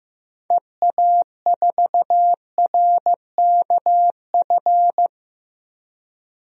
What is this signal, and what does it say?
Morse code, heard as a keyed tone, about 15 wpm, 700 Hz. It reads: EA4RKF
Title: DXCC-100 Call Signs - 6 Characters - Repeated 3x - 15wpm